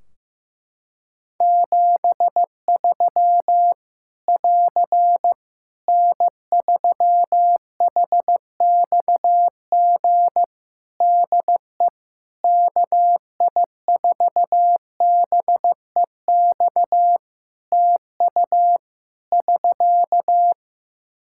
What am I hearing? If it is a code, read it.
Morse code, heard as a keyed tone, about 16 wpm, 700 Hz. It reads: 73 <AR> N3HXG DE KI4BEX TU <SK>
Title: K1USN SST QSO Practice 15wpm